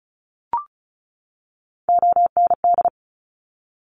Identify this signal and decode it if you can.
Morse code, heard as a keyed tone, about 36 wpm, 700 Hz. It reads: OND
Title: Top Three Letter Combinations in Words 35wpm